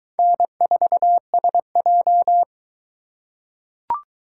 Morse code, heard as a keyed tone, about 22 wpm, 700 Hz. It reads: N4SJ